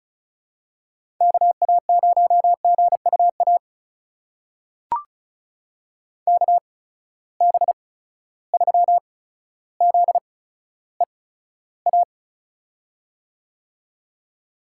Morse code, heard as a keyed tone, about 36 wpm, spaced as for 15 wpm, 700 Hz. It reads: KA0GUA KB3ZEA
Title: USA Call Signs - 6 Characters - 8x Character Spacing - 35wpm